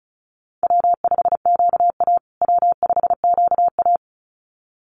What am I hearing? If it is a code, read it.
Morse code, heard as a keyed tone, about 34 wpm, 700 Hz. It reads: W5QA W5QA
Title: USA Call Signs Repeated 2x 35wpm